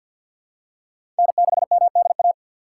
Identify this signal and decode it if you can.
Morse code, heard as a keyed tone, about 51 wpm, 700 Hz. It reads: N6MDA
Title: USA Call Signs - 5 Characters - 2x Character Spacing - 50wpm